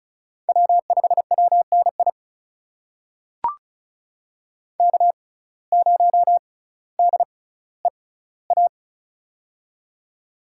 Morse code, heard as a keyed tone, about 34 wpm, spaced as for 18 wpm, 700 Hz. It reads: W5WNI K0DEA
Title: USA Call Signs - 5 Characters - 6x Character Spacing - 35wpm